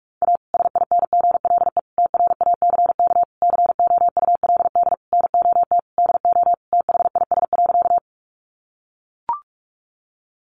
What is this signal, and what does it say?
Morse code, heard as a keyed tone, about 44 wpm, 700 Hz. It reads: A SINGLE TRACK COULD NOT DO THIS.
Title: Sentences from Top 1100 Words 45wpm